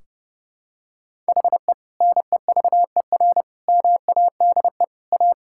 Morse code, heard as a keyed tone, about 32 wpm, 700 Hz. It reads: HE NEVER MADE A
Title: Sentences from Top 1600 Words 30wpm